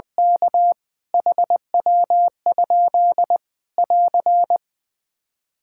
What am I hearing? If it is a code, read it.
Morse code, heard as a keyed tone, about 20 wpm, 700 Hz. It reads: K HW? <AR>
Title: K1USN SST QSO Practice 20wpm